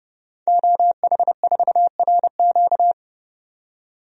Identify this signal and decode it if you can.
Morse code, heard as a keyed tone, about 30 wpm, 700 Hz. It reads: OH4RQ
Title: DXCC-100 Call Signs - 5 Characters - 30wpm